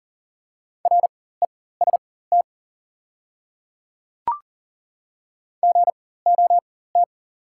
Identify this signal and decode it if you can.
Morse code, heard as a keyed tone, about 40 wpm, spaced as for 19 wpm, 700 Hz. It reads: REST GOT
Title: 4x Character Spacing - Top 500 Words - 40wpm